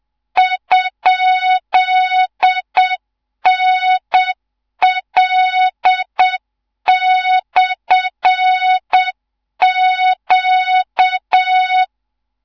Morse code, heard as a keyed tone, about 7 wpm, 3700 Hz, with noise. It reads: ?NL/Q
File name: Morze-HA2MN / hang / 14 280 HA2MN